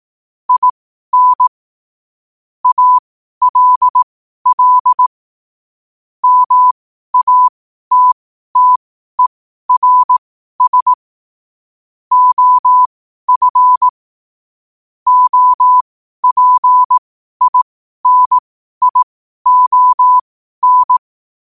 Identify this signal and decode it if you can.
Morse code, heard as a keyed tone, about 18 wpm, spaced as for 12 wpm, 1000 Hz. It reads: IN ALL MATTERS OF OPINION
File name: qotdfiles / 12wpm